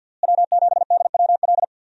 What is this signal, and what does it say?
Morse code, heard as a keyed tone, about 46 wpm, 700 Hz. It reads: W7DWL
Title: USA Call Signs - 5 Characters - Repeated 2x - 50wpm